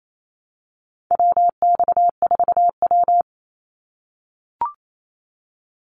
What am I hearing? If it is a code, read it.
Morse code, heard as a keyed tone, about 28 wpm, 700 Hz. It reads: WX4W
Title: Call Signs - 28wpm